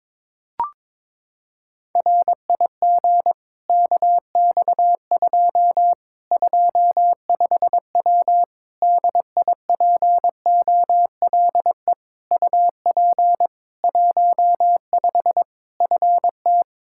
Morse code, heard as a keyed tone, about 22 wpm, 700 Hz. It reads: RIG KX2 25W DIPOLE UP 15 FT
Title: Practice QSO - Exchange 2 - Info - 22wpm